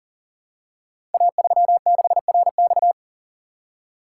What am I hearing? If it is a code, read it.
Morse code, heard as a keyed tone, about 41 wpm, 700 Hz. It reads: A36RX
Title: Code Groups - 5 Letter:Numbers - 40wpm